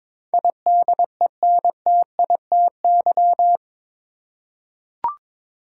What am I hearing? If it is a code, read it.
Morse code, heard as a keyed tone, about 22 wpm, 700 Hz. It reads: IDENTITY
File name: practice / 22wpm / Eight Letter Words - 22wpm